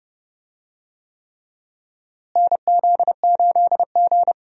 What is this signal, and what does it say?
Morse code, heard as a keyed tone, about 30 wpm, 700 Hz. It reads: NZ8G
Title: USA Call Signs - 4 Characters - Repeated 3x - 30wpm